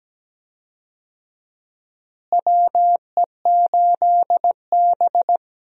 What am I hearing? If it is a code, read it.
Morse code, heard as a keyed tone, about 17 wpm, 700 Hz. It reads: WE8B
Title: USA Call Signs - 4 Characters - Repeated 2x - 17wpm